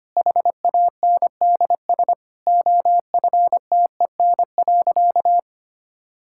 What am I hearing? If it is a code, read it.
Morse code, heard as a keyed tone, about 25 wpm, 700 Hz. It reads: HANDS OFTEN.
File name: Easy Sentences with Hints - 25wpm